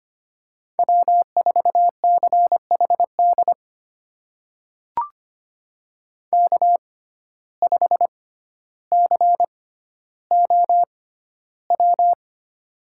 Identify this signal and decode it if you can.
Morse code, heard as a keyed tone, about 25 wpm, spaced as for 13 wpm, 700 Hz. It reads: W4CHD K5COW
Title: USA Call Signs - 5 Characters - 6x Character Spacing - 25wpm